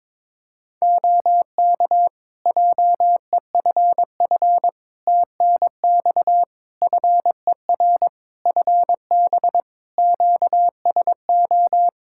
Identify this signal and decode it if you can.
Morse code, heard as a keyed tone, about 22 wpm, 700 Hz. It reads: OK JEFF TNX FER FB QSO
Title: Practice QSO - Exchange 4 - Ending - 22wpm